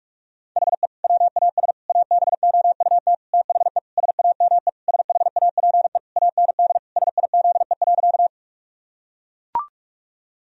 Morse code, heard as a keyed tone, about 41 wpm, 700 Hz. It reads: HE WAS ABOUT THE SAME SHAPE AND SIZE.
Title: Sentences from Top 1000 Words - Review Part 3 - 45wpm